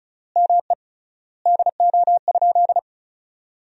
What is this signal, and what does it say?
Morse code, heard as a keyed tone, about 36 wpm, 700 Hz. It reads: ME DO?
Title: 3x Word-Spacing - Sentences from Top 200 Words - 35wpm